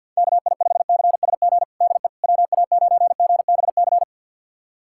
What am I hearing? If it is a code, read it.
Morse code, heard as a keyed tone, about 53 wpm, 700 Hz. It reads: KI5CSG DE WA9GBC
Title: USA Callsign DE Callsign - 6 Characters - 50wpm